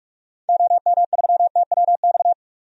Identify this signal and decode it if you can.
Morse code, heard as a keyed tone, about 44 wpm, 700 Hz. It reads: OM3TWX
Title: DXCC-100 Call Signs - 6 Characters - 45wpm